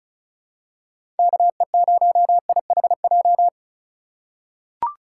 Morse code, heard as a keyed tone, about 34 wpm, 700 Hz. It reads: KE0IHJ
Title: USA Call Signs - 6 Characters - 4x Character Spacing - 35wpm